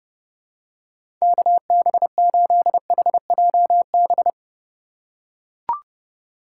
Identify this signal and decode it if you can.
Morse code, heard as a keyed tone, about 30 wpm, 700 Hz. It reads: KB8HJB
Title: USA Call Signs - 6 Characters - 3x Character Spacing - 30wpm